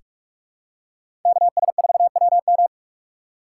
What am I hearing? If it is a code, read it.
Morse code, heard as a keyed tone, about 43 wpm, 700 Hz. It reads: KS4WM
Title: USA Call Signs - 5 Characters - 2x Character Spacing - 45wpm